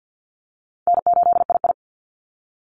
Morse code, heard as a keyed tone, about 50 wpm, 700 Hz. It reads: N8II
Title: Easy Call Signs - 50wpm